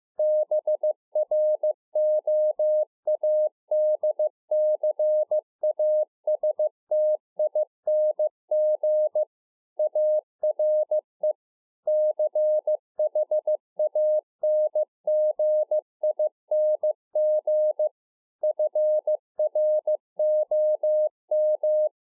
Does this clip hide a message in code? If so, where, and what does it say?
Yes: Morse code from 0.2 to 21.9 s, heard as a keyed tone, about 15 wpm, 610 Hz. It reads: BROADCASTING ARE CHANGING FROM